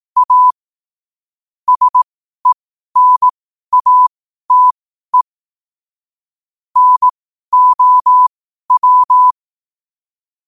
Morse code, heard as a keyed tone, about 18 wpm, spaced as for 12 wpm, 1000 Hz. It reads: A SENATE NOW
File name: qotdfiles / 12wpm